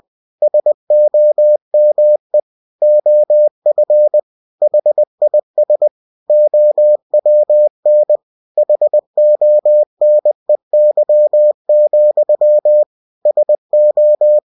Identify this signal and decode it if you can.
Morse code, heard as a keyed tone, about 20 wpm, 600 Hz. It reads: SOME OF HIS OWN HONEY, SO
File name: cw-books / winnie-the-pooh / winnie-20wpm-page-0037